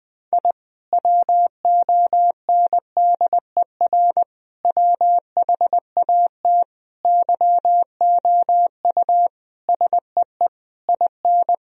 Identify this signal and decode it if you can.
Morse code, heard as a keyed tone, about 21 wpm, 700 Hz. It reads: I WONDER WHAT YOU SEE IN
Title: Sentences from Top 1000 Words - Review Part 3 - 20wpm